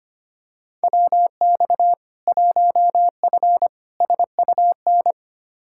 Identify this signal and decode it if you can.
Morse code, heard as a keyed tone, about 25 wpm, 700 Hz. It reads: WX 1F SUN
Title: Practice QSO - Exchange 2 - Info - 25wpm